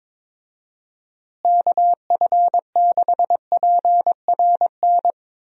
Morse code, heard as a keyed tone, about 22 wpm, 700 Hz. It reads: KF6PRN
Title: USA Call Signs - 6 Characters - Repeated 2x - 22wpm